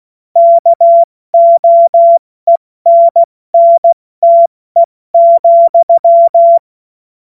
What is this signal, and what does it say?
Morse code, heard as a keyed tone, about 16 wpm, 680 Hz. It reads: KOENNTE,
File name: Morsekurs / Witze / Witz 25. Oktober 2022 70 F80